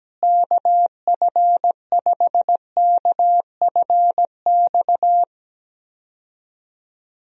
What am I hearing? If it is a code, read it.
Morse code, heard as a keyed tone, about 17 wpm, 700 Hz. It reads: KF5KFX
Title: USA Call Signs - 6 Characters - Repeated 3x - 17wpm